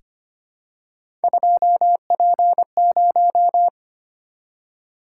Morse code, heard as a keyed tone, about 25 wpm, 700 Hz. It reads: 2P0
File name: Three Letter-Number Combinations - 25wpm